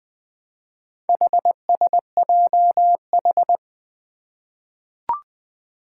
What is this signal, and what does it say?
Morse code, heard as a keyed tone, about 20 wpm, 700 Hz. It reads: HSJH
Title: Four Challenging Letter-Number Combinations 20wpm